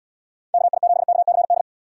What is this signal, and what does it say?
Morse code, heard as a keyed tone, about 79 wpm, 700 Hz. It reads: KI7KCD